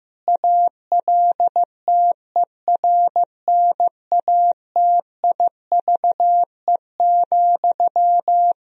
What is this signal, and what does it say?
Morse code, heard as a keyed tone, about 15 wpm, 700 Hz. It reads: ALTERNATIVE,
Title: Sentences from Top 1000 Words - Review Part 4 - 15wpm